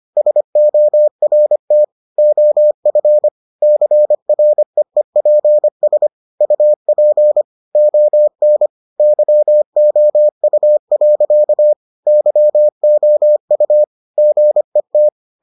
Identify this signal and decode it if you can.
Morse code, heard as a keyed tone, about 25 wpm, 600 Hz. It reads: SORT OF CREEPS UP ON YOU. YOU GET